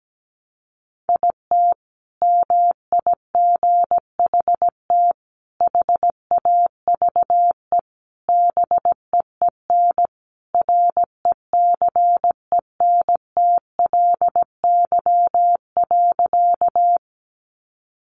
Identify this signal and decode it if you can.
Morse code, heard as a keyed tone, about 17 wpm, 700 Hz. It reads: IT MIGHT HAVE BEEN RECENTLY.
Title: Sentences from Top 1200 Words 17wpm